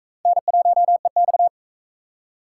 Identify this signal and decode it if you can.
Morse code, heard as a keyed tone, about 41 wpm, 700 Hz. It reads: N1EX